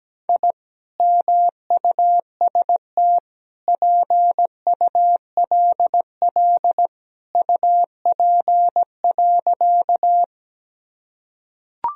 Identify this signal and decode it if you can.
Morse code, heard as a keyed tone, about 17 wpm, 700 Hz. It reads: I MUST PULL UP.
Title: Sentences from Top 1000 Words - Review Part 2 - 17wpm